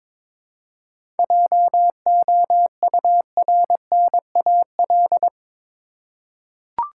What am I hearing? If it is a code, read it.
Morse code, heard as a keyed tone, about 22 wpm, 700 Hz. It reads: JOURNAL
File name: Difficult Words with Hints - 22wpm